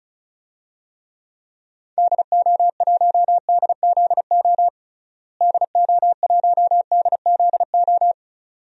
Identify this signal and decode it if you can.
Morse code, heard as a keyed tone, about 34 wpm, 700 Hz. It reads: DO1DZO DO1DZO
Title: DXCC-100 Call Signs - 6 Characters - Repeated 2x - 35wpm